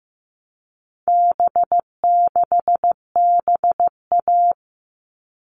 Morse code, heard as a keyed tone, about 15 wpm, 700 Hz. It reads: B6BA
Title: All Cut Numbers - 4-Digit 15wpm